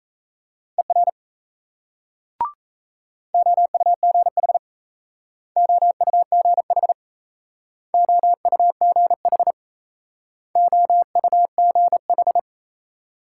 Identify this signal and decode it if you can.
Morse code, heard as a keyed tone, about 38 wpm, 700 Hz. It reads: ER OUGH OUGH OUGH OUGH